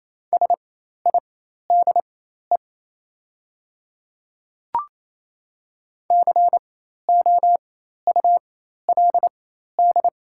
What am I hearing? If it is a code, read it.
Morse code, heard as a keyed tone, about 28 wpm, spaced as for 13 wpm, 690 Hz. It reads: SIDE COULD
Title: 4x Character Spacing - Top 200 Words - 28wpm